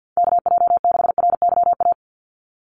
Morse code, heard as a keyed tone, about 50 wpm, 700 Hz. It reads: KJ6RYA
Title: USA Call Signs 50wpm